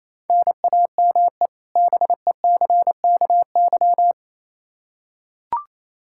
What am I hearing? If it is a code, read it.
Morse code, heard as a keyed tone, about 28 wpm, 700 Hz. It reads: NAME BECKY
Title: K1USN SST QSO Practice 28wpm